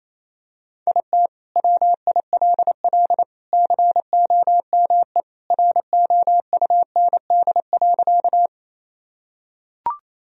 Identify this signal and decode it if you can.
Morse code, heard as a keyed tone, about 28 wpm, 700 Hz. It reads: IT WILL COME ROUND.
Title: Sentences from Top 1000 Words - Review Part 1 - 28wpm